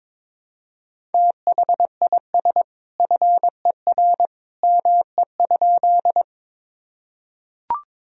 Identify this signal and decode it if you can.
Morse code, heard as a keyed tone, about 22 wpm, 700 Hz. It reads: THIS FER ME?
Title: Sentences from Top 200 Words and Abbreviations 22wpm